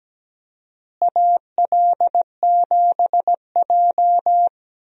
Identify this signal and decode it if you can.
Morse code, heard as a keyed tone, about 17 wpm, 700 Hz. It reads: AL7J
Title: USA Call Signs - 4 Characters - Repeated 2x - 17wpm